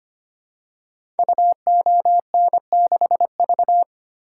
Morse code, heard as a keyed tone, about 25 wpm, 700 Hz. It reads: UON6V
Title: Code Groups - 5 Letter:Numbers - 25wpm